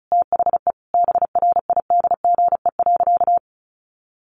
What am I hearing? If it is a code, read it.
Morse code, heard as a keyed tone, about 34 wpm, 700 Hz. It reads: THE BRIDGE.
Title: Sentences from Top 1200 Words 35wpm